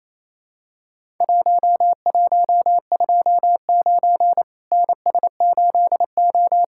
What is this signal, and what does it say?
Morse code, heard as a keyed tone, about 28 wpm, 700 Hz. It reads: 1129 NS8O
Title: Worked All Europe DX Contest - QTC Exchange - 28wpm